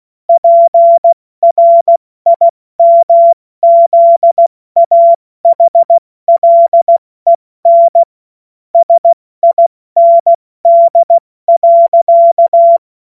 Morse code, heard as a keyed tone, about 16 wpm, 680 Hz. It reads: PRIMZAHLEN SIND.
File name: Morsekurs / Witze / Witz 12. April 2022 70 F80